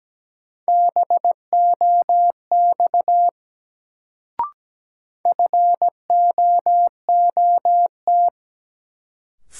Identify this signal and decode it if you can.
Morse code, heard as a keyed tone, about 17 wpm, 700 Hz. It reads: BOX FOOT
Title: Sentences from Top 500 Words - New Word Drill - 17wpm